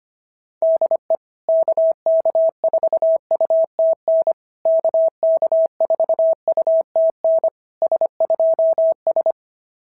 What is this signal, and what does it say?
Morse code, heard as a keyed tone, about 25 wpm, 690 Hz. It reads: DE KK4UTN KK4UTN S2S
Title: SOTA QSO Practice 25wpm